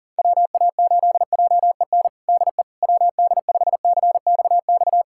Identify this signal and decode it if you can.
Morse code, heard as a keyed tone, about 40 wpm, 700 Hz. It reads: WA8JEN DE WD5CXX
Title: USA Callsign DE Callsign 40wpm